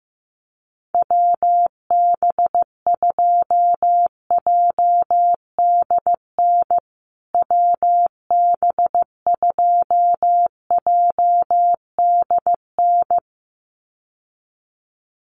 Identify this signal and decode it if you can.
Morse code, heard as a keyed tone, about 15 wpm, 700 Hz. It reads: WB2JDN WB2JDN
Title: USA Call Signs Repeated 2x 15wpm